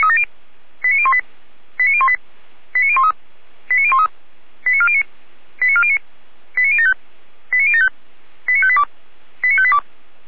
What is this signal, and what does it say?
Five-tone selective call (ZVEI), heard as quick runs of five tones, tones of about 70 ms. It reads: 89018 89018 89013 89013 89409 89409 89075 89075 89582 89582
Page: SDR/ZVEI